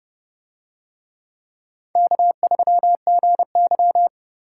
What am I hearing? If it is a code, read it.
Morse code, heard as a keyed tone, about 30 wpm, 700 Hz. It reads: K3GY